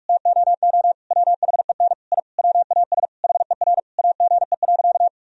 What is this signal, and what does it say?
Morse code, heard as a keyed tone, about 46 wpm, 700 Hz. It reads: TOO WHEN I WAS HER AGE.
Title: Sentences from Top 1000 Words - Review Part 3 - 45wpm